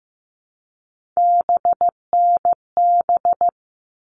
Morse code, heard as a keyed tone, about 15 wpm, 700 Hz. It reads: BNB
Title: All Cut Numbers - 3-Digit 15wpm